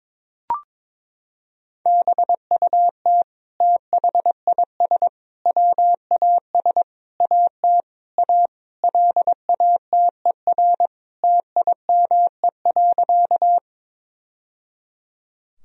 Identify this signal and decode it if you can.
Morse code, heard as a keyed tone, about 22 wpm, 700 Hz. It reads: BUT THIS WAS AT A LATER TIME.
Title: Sentences from Top 1000 Words - Review Part 4 - 22wpm